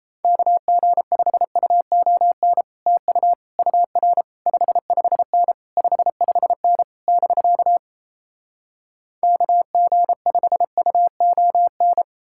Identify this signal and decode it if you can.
Morse code, heard as a keyed tone, about 37 wpm, 700 Hz. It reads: KG5UON TU UR 55N 55N <BK> KG5UON